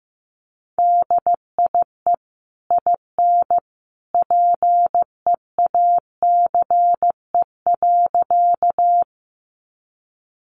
Morse code, heard as a keyed tone, about 15 wpm, 700 Hz. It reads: DIE IN PEACE.
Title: Sentences from Top 1500 Words 15wpm